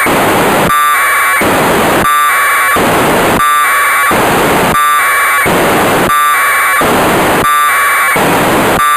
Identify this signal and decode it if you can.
Packet radio (AX.25, as in APRS).
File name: Negai 1200bd packet received by JA6PL provided by JA0CAW